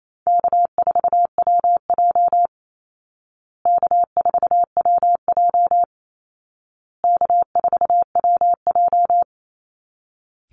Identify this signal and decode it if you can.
Morse code, heard as a keyed tone, about 28 wpm, 700 Hz. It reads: K4WJ K4WJ K4WJ
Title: Call Signs Repeated 2x - 28wpm